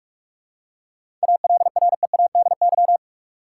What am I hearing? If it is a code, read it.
Morse code, heard as a keyed tone, about 43 wpm, 700 Hz. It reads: ALREADY
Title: Seven Letter Words - 45wpm